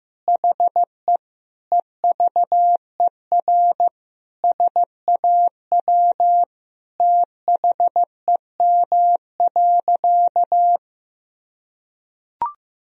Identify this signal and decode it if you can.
Morse code, heard as a keyed tone, about 15 wpm, 700 Hz. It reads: HE EVER SAW THEM.
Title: Sentences from Top 1000 Words - Review Part 2 - 15wpm